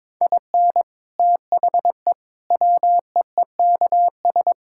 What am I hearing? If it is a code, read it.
Morse code, heard as a keyed tone, about 22 wpm, 700 Hz. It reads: IN THE WEEKS
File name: practice / 22wpm / Sets of 7 Words - 22wpm